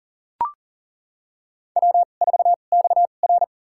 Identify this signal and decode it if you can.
Morse code, heard as a keyed tone, about 37 wpm, spaced as for 28 wpm, 700 Hz. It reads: W4XR
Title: USA Call Signs - 4 Characters - 2x Character Spacing - 40wpm